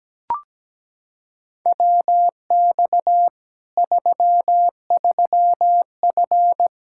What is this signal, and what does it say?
Morse code, heard as a keyed tone, about 17 wpm, 700 Hz. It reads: WX 33F
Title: Practice QSO - Exchange 2 - Info - 17wpm